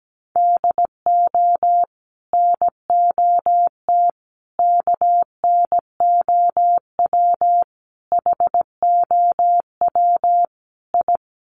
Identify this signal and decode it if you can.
Morse code, heard as a keyed tone, about 17 wpm, 700 Hz. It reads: DO NOT KNOW HOW I
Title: Sets of 5 Words Encore2 17wpm